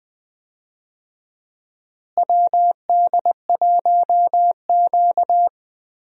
Morse code, heard as a keyed tone, about 20 wpm, 700 Hz. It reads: WD1Q
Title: USA Call Signs - 4 Characters - Repeated 2x - 20wpm